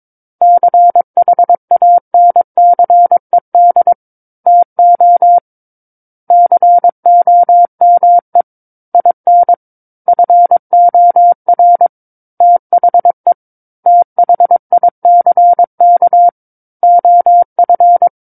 Morse code, heard as a keyed tone, about 22 wpm, 700 Hz. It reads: CHANCED TO COME IN FOR THE THICK OF
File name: Book One / 13-Chapter 13 - 22 WPM